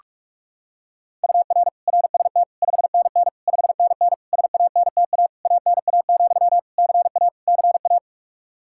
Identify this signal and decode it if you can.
Morse code, heard as a keyed tone, about 48 wpm, 700 Hz. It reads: UR RST 5NN 5NN SANTA ANA, CA CA
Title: Practice QSO - Exchange 1 - Beginning - 45wpm